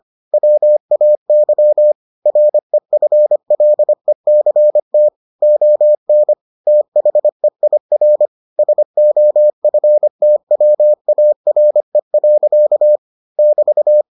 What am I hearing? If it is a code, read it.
Morse code, heard as a keyed tone, about 25 wpm, 600 Hz. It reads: WAY REFLECT ON THEIR SOFTWARE. <BT>